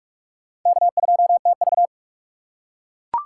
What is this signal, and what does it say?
Morse code, heard as a keyed tone, about 42 wpm, 700 Hz. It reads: K2TV
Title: K1USN SST QSO Practice - 45wpm